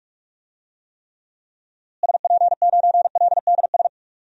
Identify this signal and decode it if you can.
Morse code, heard as a keyed tone, about 44 wpm, 700 Hz. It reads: SP9LDS
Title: DXCC-100 Call Signs - 6 Characters - Repeated 2x - 45wpm